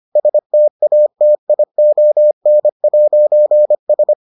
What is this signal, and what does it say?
Morse code, heard as a keyed tone, about 25 wpm, 600 Hz. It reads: STATION'S